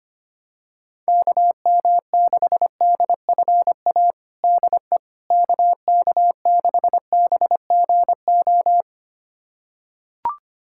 Morse code, heard as a keyed tone, about 25 wpm, 700 Hz. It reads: KM6DFA DE KK6BGO
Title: USA Callsign DE Callsign - 6 Characters - 25wpm